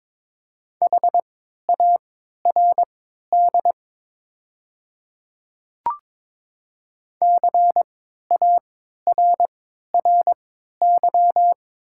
Morse code, heard as a keyed tone, about 23 wpm, spaced as for 13 wpm, 690 Hz. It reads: HARD CARRY
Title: 3x Character Spacing - Top 300 Words - 22wpm